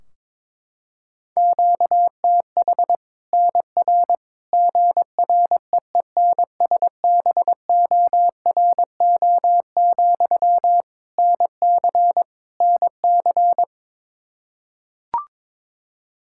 Morse code, heard as a keyed tone, about 22 wpm, 700 Hz. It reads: QTH NR GREENSBORO, NC NC
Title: Practice QSO - Exchange 1 - Beginning - 22wpm